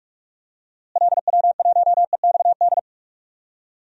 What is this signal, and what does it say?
Morse code, heard as a keyed tone, about 46 wpm, 700 Hz. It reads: LW1EXD
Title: DXCC-100 Call Signs - 6 Characters - Repeated 2x - 45wpm